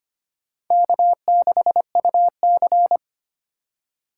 Morse code, heard as a keyed tone, about 25 wpm, 700 Hz. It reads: K6UC
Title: USA Call Signs - 4 Characters - 2x Character Spacing - 25wpm